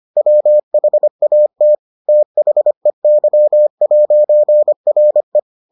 Morse code, heard as a keyed tone, about 25 wpm, 600 Hz. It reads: WHAT THEY'RE